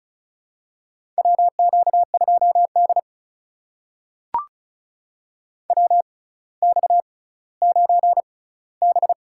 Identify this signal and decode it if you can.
Morse code, heard as a keyed tone, about 36 wpm, spaced as for 18 wpm, 700 Hz. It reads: WQ2D WX9B
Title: USA Call Signs - 4 Characters - 6x Character Spacing - 35wpm